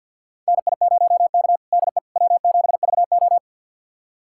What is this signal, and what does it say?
Morse code, heard as a keyed tone, about 46 wpm, 700 Hz. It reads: NI0K DE W7VO
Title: USA Callsign DE Callsign - 4 Characters - 50wpm